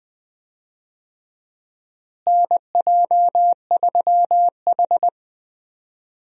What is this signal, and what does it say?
Morse code, heard as a keyed tone, about 20 wpm, 700 Hz. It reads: NJ3H
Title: USA Call Signs - 4 Characters - Repeated 3x - 20wpm